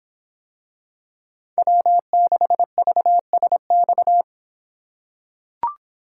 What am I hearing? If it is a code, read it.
Morse code, heard as a keyed tone, about 26 wpm, 700 Hz. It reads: W6VSX